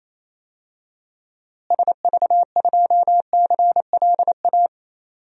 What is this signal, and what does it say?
Morse code, heard as a keyed tone, about 28 wpm, 700 Hz. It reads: SV2CLA
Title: DXCC-100 Call Signs - 6 Characters - Repeated 2x - 28wpm